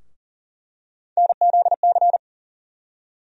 Morse code, heard as a keyed tone, about 40 wpm, 700 Hz. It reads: NZC
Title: USA Call Sign Postfixes - 3 Letters 40wpm